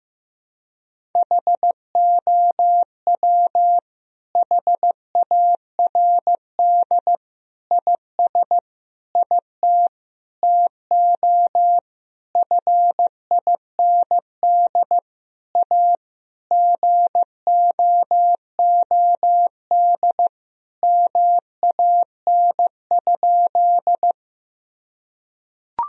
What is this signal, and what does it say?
Morse code, heard as a keyed tone, about 14 wpm, 700 Hz. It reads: HOW HARD IS IT TO FIND A GOOD MAN?
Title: Sentences from Top 1000 Words - Review Part 1 - 15wpm